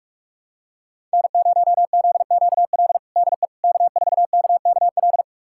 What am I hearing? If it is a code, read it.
Morse code, heard as a keyed tone, about 43 wpm, 700 Hz. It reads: N0ZQL DE K4KKL
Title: USA Callsign DE Callsign - 5 Characters - 45wpm